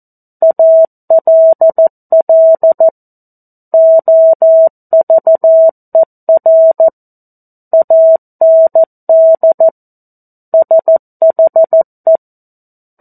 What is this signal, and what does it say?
Morse code, heard as a keyed tone, about 14 wpm, 650 Hz. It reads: ALL OVER AND SHE
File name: 02-Chapter-02 14 WPM